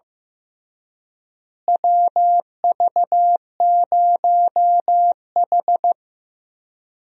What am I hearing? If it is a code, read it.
Morse code, heard as a keyed tone, about 15 wpm, 700 Hz. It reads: WV0H
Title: USA Call Signs - 4 Characters - Repeated 3x - 15wpm